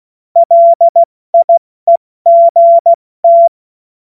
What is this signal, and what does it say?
Morse code, heard as a keyed tone, about 16 wpm, 680 Hz. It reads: LIEGT
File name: Morsekurs / Witze / Witz 15. Juni 2021, 70 F80